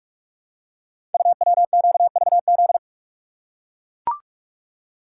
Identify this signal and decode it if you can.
Morse code, heard as a keyed tone, about 44 wpm, 690 Hz. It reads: UWQVZ
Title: Code Groups - 5 Letters - 45wpm